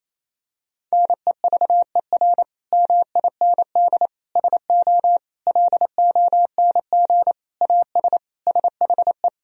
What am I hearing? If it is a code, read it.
Morse code, heard as a keyed tone, about 28 wpm, 700 Hz. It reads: NEVER MIND SO LONG AS SHE
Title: Sentences from Top 1000 Words - Review Part 1 - 28wpm